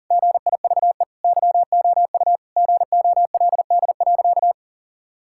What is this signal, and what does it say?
Morse code, heard as a keyed tone, about 40 wpm, 700 Hz. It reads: GIVE YOU GOLD.
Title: Sentences from Top 1000 Words - Review Part 6 - 40wpm